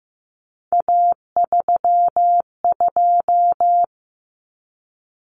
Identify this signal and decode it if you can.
Morse code, heard as a keyed tone, about 15 wpm, 700 Hz. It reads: A32
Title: Common Cut Numbers - 3-Digit 15wpm